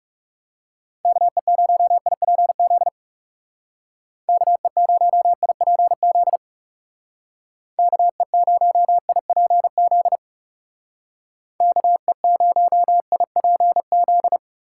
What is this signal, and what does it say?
Morse code, heard as a keyed tone, about 38 wpm, 700 Hz. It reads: KE0IPZ KE0IPZ KE0IPZ KE0IPZ